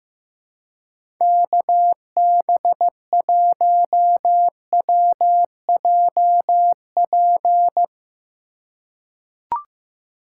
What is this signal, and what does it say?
Morse code, heard as a keyed tone, about 16 wpm, 700 Hz. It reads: KB1WJP
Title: USA Call Signs - 6 Characters - 2x Character Spacing - 15wpm